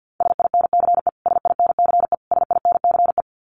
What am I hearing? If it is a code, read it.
Morse code, heard as a keyed tone, about 50 wpm, 700 Hz. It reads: SINCE SINCE SINCE
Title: Top 500 Words Repeated 50wpm